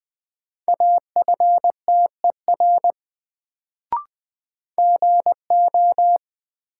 Morse code, heard as a keyed tone, about 20 wpm, 700 Hz. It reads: AFTER GO
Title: Sets of 2 Words - New Word Drill - 20wpm